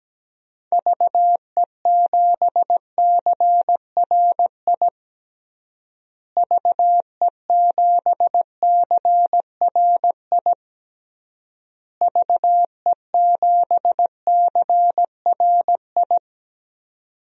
Morse code, heard as a keyed tone, about 17 wpm, 700 Hz. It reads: VE7CRI VE7CRI VE7CRI
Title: DXCC-100 Call Signs - 6 Characters - Repeated 3x - 17wpm